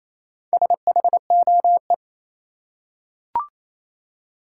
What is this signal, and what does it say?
Morse code, heard as a keyed tone, about 28 wpm, 700 Hz. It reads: SHOE
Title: Four Letter Words - 28wpm